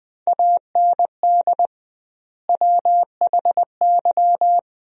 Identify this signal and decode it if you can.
Morse code, heard as a keyed tone, about 20 wpm, 700 Hz. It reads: AND WHY
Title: 2x Word-Spacing - Sentences from Top 400 Words - 20wpm